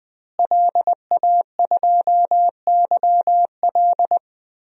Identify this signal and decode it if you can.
Morse code, heard as a keyed tone, about 20 wpm, 700 Hz. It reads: LA2YL
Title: DXCC-100 Call Signs - 5 Characters - Repeated 2x - 20wpm